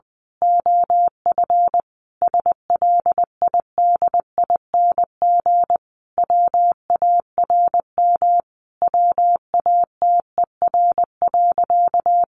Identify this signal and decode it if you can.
Morse code, heard as a keyed tone, about 20 wpm, 700 Hz. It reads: OF SLIDING WARM WATER.
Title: A Sound Of Thunder 20wpm